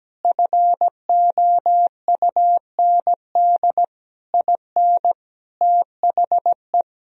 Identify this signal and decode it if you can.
Morse code, heard as a keyed tone, about 17 wpm, 700 Hz. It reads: FOUND IN THE
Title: Sentences from Top 1000 Words - Review Part 1 - 17wpm